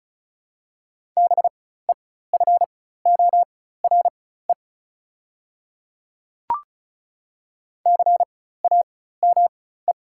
Morse code, heard as a keyed tone, about 36 wpm, spaced as for 16 wpm, 700 Hz. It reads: BEFORE CAME
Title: 4x Character Spacing - Top 200 Words - 35wpm